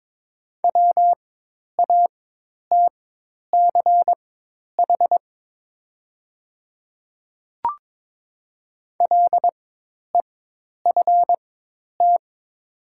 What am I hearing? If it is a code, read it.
Morse code, heard as a keyed tone, about 22 wpm, spaced as for 10 wpm, 700 Hz. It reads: WATCH LEFT
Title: 4x Character Spacing - Top 500 Words - 22wpm